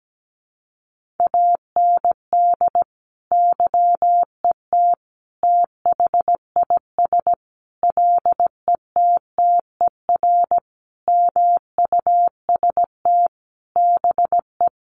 Morse code, heard as a keyed tone, about 17 wpm, 700 Hz. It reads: AND YET THIS LETTER MUST BE
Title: Sentences from Top 1400 Words 17wpm